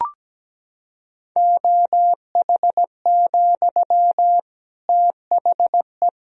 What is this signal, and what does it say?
Morse code, heard as a keyed tone, about 17 wpm, 700 Hz. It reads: OH, THE
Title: Sentences from Top 1000 Words - Review Part 3 - 17wpm